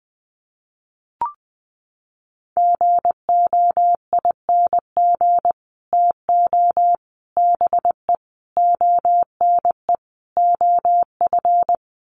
Morse code, heard as a keyed tone, about 20 wpm, 700 Hz. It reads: GOING TO BE ONE OF
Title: Sets of 5 Words Encore2 20wpm